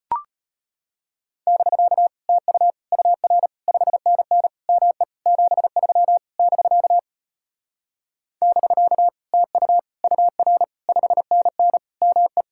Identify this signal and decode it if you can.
Morse code, heard as a keyed tone, about 33 wpm, 700 Hz. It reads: <BK> TU UR 5NN ME 73 <BK> <BK> TU UR 5NN ME